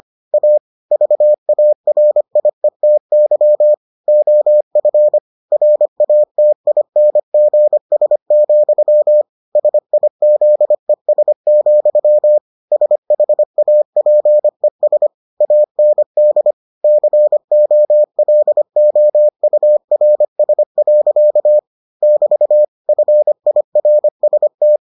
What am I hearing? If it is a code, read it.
Morse code, heard as a keyed tone, about 25 wpm, 600 Hz. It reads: A VARIETY OF RATINGS, SIZES, SHAPES AND COLOURS. <BT> FIRST